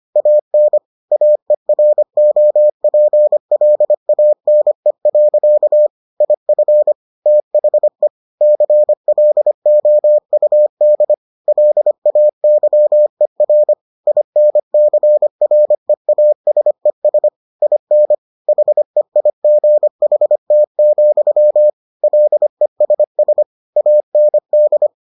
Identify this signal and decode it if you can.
Morse code, heard as a keyed tone, about 25 wpm, 600 Hz. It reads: AN AEROPLANE. IF THE CLOUD LAYER INCREASES IN HEIGHT, LESS AND